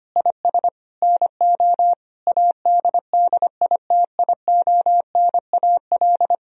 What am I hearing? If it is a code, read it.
Morse code, heard as a keyed tone, about 25 wpm, 690 Hz. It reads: IS NO ADDITIONAL
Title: Sentences from Top 1000 Words - Review Part 4 - 25wpm